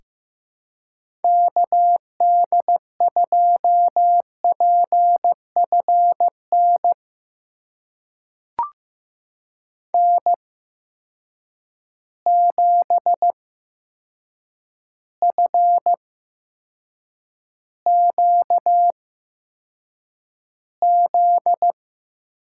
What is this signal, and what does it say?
Morse code, heard as a keyed tone, about 15 wpm, spaced as for 7 wpm, 700 Hz. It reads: KD2PFN N7FQZ
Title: USA Call Signs - 8x Character Spacing - 15wpm